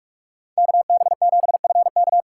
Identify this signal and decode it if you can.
Morse code, heard as a keyed tone, about 41 wpm, 700 Hz. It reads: KB7FK
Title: USA Call Signs - 5 Characters - 2x Character Spacing - 45wpm